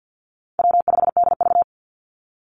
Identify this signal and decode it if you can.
Morse code, heard as a keyed tone, about 50 wpm, 700 Hz. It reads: W5DV
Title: Easy Call Signs - 50wpm